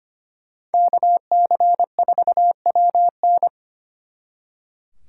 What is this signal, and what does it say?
Morse code, heard as a keyed tone, about 32 wpm, 700 Hz. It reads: KC4WN